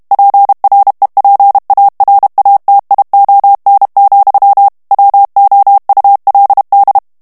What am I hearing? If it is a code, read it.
Morse code, heard as a keyed tone, about 32 wpm, 800 Hz. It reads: PREPARATION, WOULD
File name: morse mp3 / ANGLAIS texte clair 3 vitesse 30 mots minute